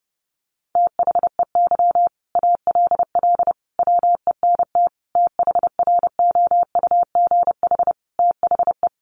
Morse code, heard as a keyed tone, about 30 wpm, 700 Hz. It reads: THEY ALL WENT THROUGH THE
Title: Sentences from Top 1300 Words 30wpm